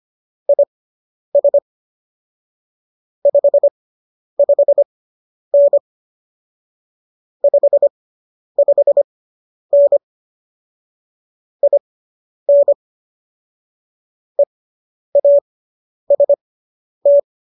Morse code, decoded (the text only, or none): IS 55N 55N IN EAST